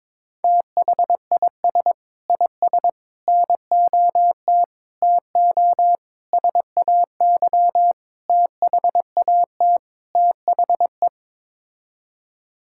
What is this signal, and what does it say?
Morse code, heard as a keyed tone, about 22 wpm, 700 Hz. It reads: THIS IS NOT TO SAY THAT THE
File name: practice / 22wpm / Sets of 7 Words - 22wpm